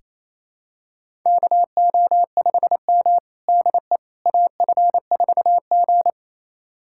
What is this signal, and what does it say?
Morse code, heard as a keyed tone, about 28 wpm, 700 Hz. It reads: KO5M DE AF4G
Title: USA Callsign DE Callsign - 4 Characters - 28wpm